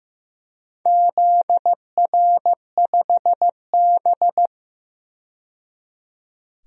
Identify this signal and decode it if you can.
Morse code, heard as a keyed tone, about 15 wpm, 700 Hz. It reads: ZR5B
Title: DXCC-100 Call Signs - 4 Characters - 15wpm